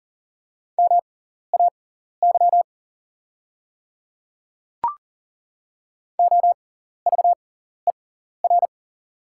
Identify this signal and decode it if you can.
Morse code, heard as a keyed tone, about 40 wpm, spaced as for 14 wpm, 700 Hz. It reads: MAY OVER
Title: 6x Character Spacing - Top 100 Words - 40wpm